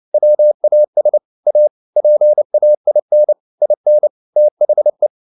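Morse code, heard as a keyed tone, about 29 wpm, 600 Hz. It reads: WAS A PAIN IN THE
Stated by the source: Morse code at 29 wpm, sending Ken Follett's novel Never in English, from CW med Gnister.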